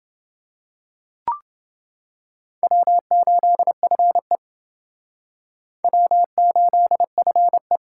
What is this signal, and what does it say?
Morse code, heard as a keyed tone, about 28 wpm, 700 Hz. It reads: W8FE W8FE